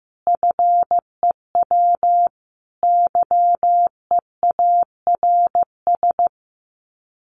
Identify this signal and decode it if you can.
Morse code, heard as a keyed tone, about 15 wpm, 700 Hz. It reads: FEW YEARS
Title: Sets of 5 Words Encore 15wpm